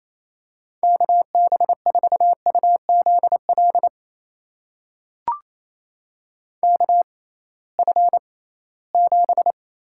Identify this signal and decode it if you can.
Morse code, heard as a keyed tone, about 28 wpm, spaced as for 18 wpm, 700 Hz. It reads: KB4UZL KF7
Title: USA Call Signs - 6 Characters - 6x Character Spacing - 28wpm